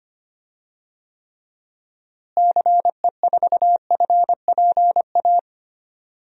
Morse code, heard as a keyed tone, about 25 wpm, 700 Hz. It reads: CE4FPA
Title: DXCC-100 Call Signs - 6 Characters - Repeated 2x - 25wpm